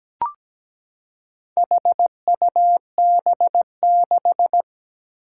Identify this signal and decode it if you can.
Morse code, heard as a keyed tone, about 17 wpm, 700 Hz. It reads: HUB6
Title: Four Challenging Letter-Number Combinations 17wpm